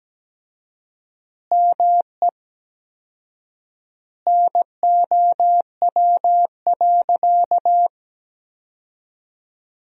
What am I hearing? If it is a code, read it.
Morse code, heard as a keyed tone, about 17 wpm, 700 Hz. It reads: ME NOW.
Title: 4x Word-Spacing - Sentences from Top 200 Words - 17wpm